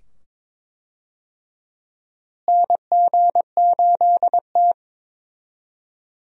Morse code, heard as a keyed tone, about 22 wpm, 700 Hz. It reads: NG8T
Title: USA Call Signs - 4 Characters - Repeated 2x - 22wpm